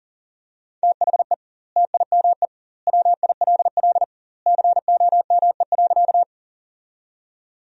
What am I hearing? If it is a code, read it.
Morse code, heard as a keyed tone, about 40 wpm, 700 Hz. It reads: THE TIME WILL COME.
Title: 2x Word-Spacing - Sentences from Top 100 Words - 40wpm